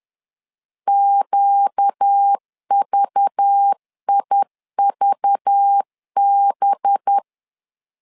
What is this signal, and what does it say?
Morse code, heard as a keyed tone, about 10 wpm, 800 Hz. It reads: QVIVB
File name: LEZIONI / 5 LBVC / 5 LBV 10wpm 2248